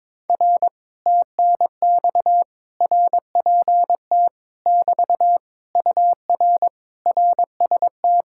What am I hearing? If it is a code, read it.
Morse code, heard as a keyed tone, about 22 wpm, 700 Hz. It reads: R TNX RPT <BT> UR RST
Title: Speed-Racing Common QSO Phrases Encore - 22wpm